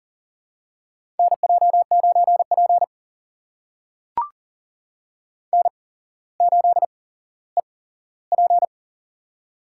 Morse code, heard as a keyed tone, about 42 wpm, spaced as for 17 wpm, 700 Hz. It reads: NJ9P N8EP
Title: USA Call Signs - 4 Characters - 8x Character Spacing - 40wpm